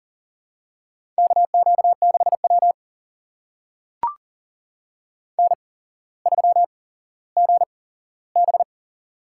Morse code, heard as a keyed tone, about 38 wpm, spaced as for 17 wpm, 700 Hz. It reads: KQ6W N3GB
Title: USA Call Signs - 4 Characters - 8x Character Spacing - 40wpm